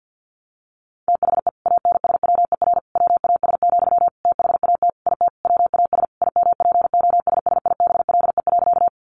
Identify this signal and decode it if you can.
Morse code, heard as a keyed tone, about 50 wpm, 700 Hz. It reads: THE ANSWER WAS, THAT IT WAS IMPOSSIBLE.
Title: Sentences from Top 1100 Words 50wpm